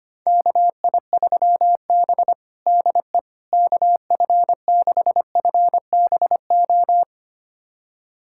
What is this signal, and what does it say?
Morse code, heard as a keyed tone, about 25 wpm, 700 Hz. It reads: KI3B DE KF6FBO
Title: USA Callsign DE Callsign 25wpm